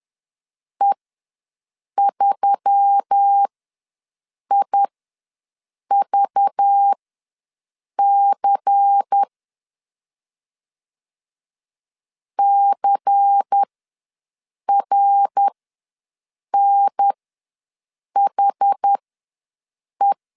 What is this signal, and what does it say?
Morse code, heard as a keyed tone, about 10 wpm, spaced as for 6 wpm, 800 Hz. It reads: E3IVC CRNHE
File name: LEZIONI / 5 LBVC / 5 LBVC ws 10wpm 3733